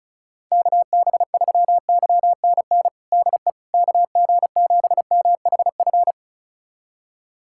Morse code, heard as a keyed tone, about 36 wpm, 700 Hz. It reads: KB3YNN DE KG7MHF
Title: USA Callsign DE Callsign - 6 Characters - 35wpm